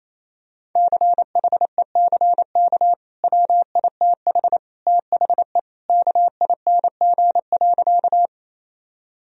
Morse code, heard as a keyed tone, about 28 wpm, 700 Hz. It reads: CHECK WITH THE KING.
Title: Sentences from Top 1000 Words - Review Part 3 - 28wpm